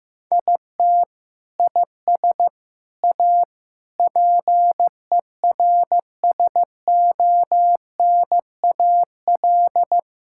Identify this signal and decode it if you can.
Morse code, heard as a keyed tone, about 15 wpm, 700 Hz. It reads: IT IS A PERSONAL
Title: Sentences from Top 1000 Words - Review Part 5 - 15wpm